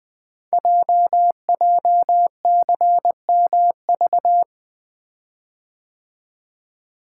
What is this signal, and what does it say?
Morse code, heard as a keyed tone, about 20 wpm, 700 Hz. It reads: JJCMV
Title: Code Groups - 5 Letters - 20wpm